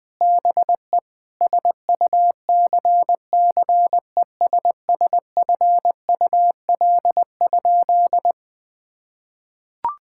Morse code, heard as a keyed tone, about 20 wpm, 700 Hz. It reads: BE SUCCESSFUL?
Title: Sentences from Top 1000 Words - Review Part 6 - 20wpm